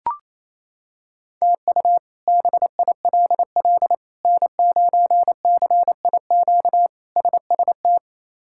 Morse code, heard as a keyed tone, about 28 wpm, 700 Hz. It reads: TU BILL N9CIQ SST
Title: K1USN SST QSO Practice - 28wpm